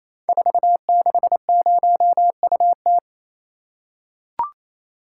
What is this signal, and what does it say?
Morse code, heard as a keyed tone, about 29 wpm, 700 Hz. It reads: 460UT
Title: Code Groups - 5 Letter:Numbers - 28wpm